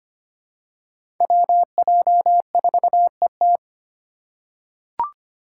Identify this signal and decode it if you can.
Morse code, heard as a keyed tone, about 25 wpm, 700 Hz. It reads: WJ4ET
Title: USA Call Signs - 5 Characters - 2x Character Spacing - 25wpm